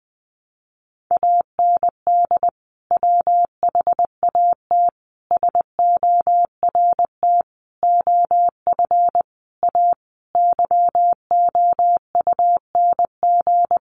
Morse code, heard as a keyed tone, about 19 wpm, 700 Hz. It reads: AND WHAT SORT OF A YOUNG
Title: Sentences from Top 1300 Words 20wpm